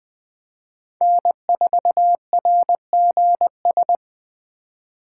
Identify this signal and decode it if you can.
Morse code, heard as a keyed tone, about 20 wpm, 700 Hz. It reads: N4RGS
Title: USA Call Signs - 5 Characters - 2x Character Spacing - 20wpm